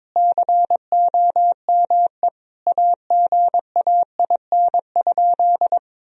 Morse code, heard as a keyed tone, about 21 wpm, 700 Hz. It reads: COME AGAIN?
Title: Sentences from Top 1000 Words - Review Part 4 - 22wpm